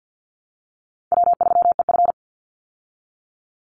Morse code, heard as a keyed tone, about 50 wpm, 700 Hz. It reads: W3EF
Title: Easy Call Signs Repeated 2x - 50wpm